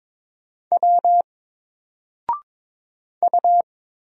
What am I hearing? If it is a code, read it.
Morse code, heard as a keyed tone, about 22 wpm, 700 Hz. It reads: W U
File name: Single-Letter-Number-22wpm